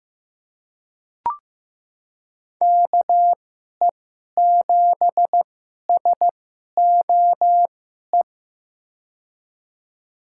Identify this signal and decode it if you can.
Morse code, heard as a keyed tone, about 15 wpm, spaced as for 11 wpm, 700 Hz. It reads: KE7SOE
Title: USA Call Signs - 2x Character Spacing - 15wpm